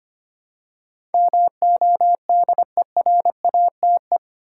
Morse code, heard as a keyed tone, about 25 wpm, 700 Hz. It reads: MODERATE
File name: 8 Letter Words - First 3 Letters Given - 25wpm